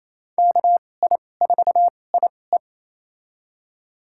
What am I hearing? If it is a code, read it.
Morse code, heard as a keyed tone, about 28 wpm, spaced as for 20 wpm, 700 Hz. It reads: KI4IE
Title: USA Call Signs - 5 Characters - 2x Character Spacing - 28wpm